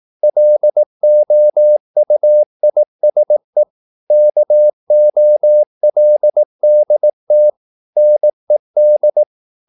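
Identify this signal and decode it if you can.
Morse code, heard as a keyed tone, about 18 wpm, 600 Hz. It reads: LOUISE KOLDT NED
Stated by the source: Slow (QRS) Morse practice at 18 wpm, CW med Gnister.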